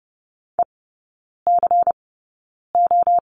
Morse code, heard as a keyed tone, about 30 wpm, spaced as for 9 wpm, 700 Hz. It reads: ECO
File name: Lesson 19e - Letter w - Three Characters 30wpm